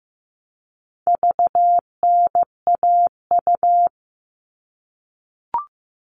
Morse code, heard as a keyed tone, about 15 wpm, 700 Hz. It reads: VNAU
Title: All Cut Numbers - 4-Digit 15wpm